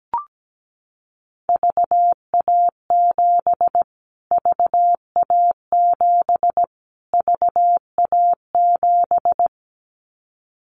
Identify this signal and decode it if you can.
Morse code, heard as a keyed tone, about 17 wpm, 700 Hz. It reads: VA7 VA7 VA7
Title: Callsign Prefixes for Top 25 DXCC - Repeated 3x - 17wpm